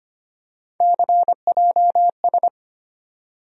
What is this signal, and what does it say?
Morse code, heard as a keyed tone, about 25 wpm, 700 Hz. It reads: CJS
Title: USA Call Sign Postfixes - 3 Letters 25wpm